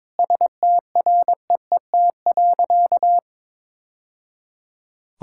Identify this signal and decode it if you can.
Morse code, heard as a keyed tone, about 22 wpm, 700 Hz. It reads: STREET.
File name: practice / 22wpm / Sentences with 2-Word Phrases - 22wpm